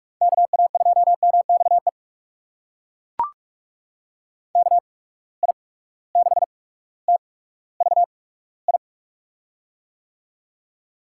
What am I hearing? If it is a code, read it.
Morse code, heard as a keyed tone, about 43 wpm, spaced as for 19 wpm, 700 Hz. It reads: KA2MXE KI6TVI
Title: USA Call Signs - 6 Characters - 8x Character Spacing - 45wpm